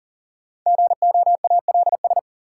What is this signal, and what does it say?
Morse code, heard as a keyed tone, about 40 wpm, 700 Hz. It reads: GOALS
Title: Sentences from Top 1000 Words - Review Part 6 - New Word Drill - 40wpm